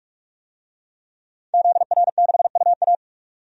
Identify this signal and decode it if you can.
Morse code, heard as a keyed tone, about 46 wpm, 700 Hz. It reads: ZR6UA
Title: DXCC-100 Call Signs - 5 Characters - 45wpm